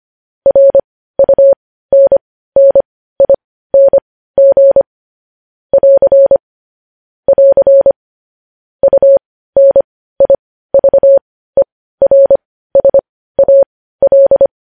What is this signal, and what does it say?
Morse code, heard as a keyed tone, about 25 wpm, spaced as for 15 wpm, 550 Hz. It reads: RUNNING <AR> <AR> UNIVERSAL
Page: news headlines in morse code at 15 wpm